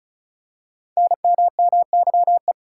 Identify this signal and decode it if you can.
Morse code, heard as a keyed tone, about 34 wpm, 700 Hz. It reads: NMMYE
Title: Code Groups - 5 Letters - 35wpm